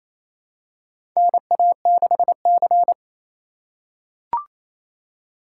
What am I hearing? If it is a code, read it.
Morse code, heard as a keyed tone, about 28 wpm, 700 Hz. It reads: NA6C
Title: USA Call Signs - 4 Characters - 3x Character Spacing - 28wpm